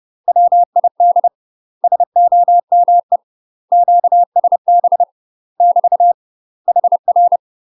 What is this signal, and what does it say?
Morse code, heard as a keyed tone, about 30 wpm, 700 Hz. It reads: WID SOME QSB <BT> HR
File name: UK MASTER QSOs TRAINING / 30WPM / UK MASTER QSOs 0027